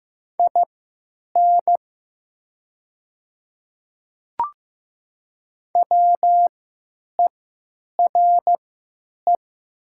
Morse code, heard as a keyed tone, about 15 wpm, spaced as for 9 wpm, 700 Hz. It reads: IN WERE
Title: 3x Character Spacing - Top 300 Words - 15wpm